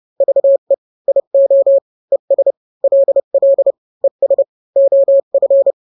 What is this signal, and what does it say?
Morse code, heard as a keyed tone, about 30 wpm, 550 Hz. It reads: VE IO ES LL ES OF
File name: n-grams / 2-grams 30 25 WPM